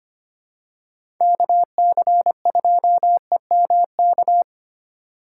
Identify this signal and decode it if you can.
Morse code, heard as a keyed tone, about 25 wpm, 700 Hz. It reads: KC2EMK
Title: USA Call Signs - 6 Characters - 2x Character Spacing - 25wpm